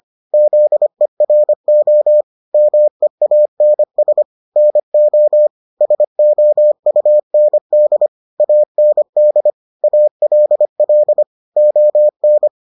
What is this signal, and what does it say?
Morse code, heard as a keyed tone, about 25 wpm, 600 Hz. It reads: ZERO MEANS NO SOUND AND ALL ON